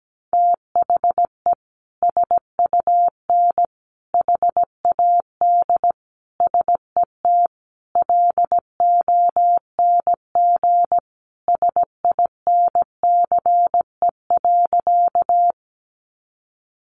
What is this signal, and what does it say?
Morse code, heard as a keyed tone, about 17 wpm, 700 Hz. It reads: THE SUN HAD SET LONG SINCE.
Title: Sentences from Top 300 Words 17wpm